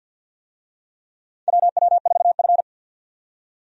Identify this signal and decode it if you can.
Morse code, heard as a keyed tone, about 53 wpm, 700 Hz. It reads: WW4F
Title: USA Call Signs - 4 Characters - 50wpm